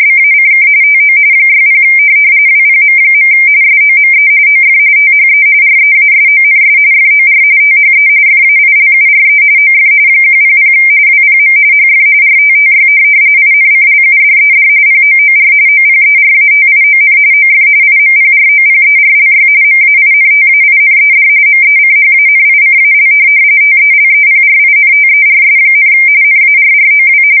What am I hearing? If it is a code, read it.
RTTY, about 45.45 baud. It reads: BACK TO 60 WPM. SOME OF THE REASONS FOR THE FAILURE OF 100 WPM HF RTTY INCLUDED POOR OPERATION OF IMPROPERLY MAINTAINED MECHANICAL TELEPRINTERS, NARROW
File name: fsk170 / teletype-RTTY